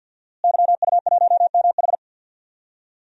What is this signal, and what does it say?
Morse code, heard as a keyed tone, about 52 wpm, 700 Hz. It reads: YU1MH
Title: DXCC-100 Call Signs - 5 Characters - Repeated 3x - 50wpm